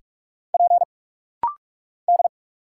Morse code, heard as a keyed tone, about 46 wpm, 690 Hz. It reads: P D
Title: Single Letters - Rapid-Fire 45wpm